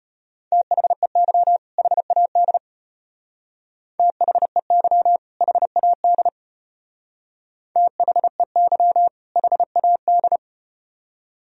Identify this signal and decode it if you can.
Morse code, heard as a keyed tone, about 33 wpm, 700 Hz. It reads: THEY HAD THEY HAD THEY HAD